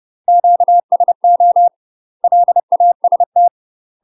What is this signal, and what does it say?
Morse code, heard as a keyed tone, about 30 wpm, 700 Hz. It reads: QSO LAST
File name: UK MASTER QSOs TRAINING / 30WPM / UK MASTER QSOs 0027